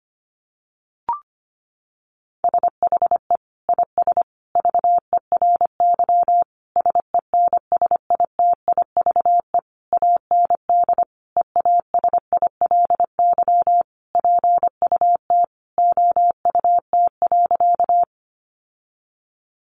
Morse code, heard as a keyed tone, about 25 wpm, 700 Hz. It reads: SHE IS VERY SENSITIVE AND EASILY PUT OUT.
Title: Sentences from Top 1400 Words 25wpm